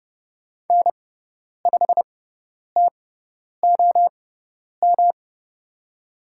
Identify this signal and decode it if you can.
Morse code, heard as a keyed tone, about 30 wpm, spaced as for 10 wpm, 700 Hz. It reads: N5TOM
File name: Lesson 18g - Letter m - Call Signs 30wpm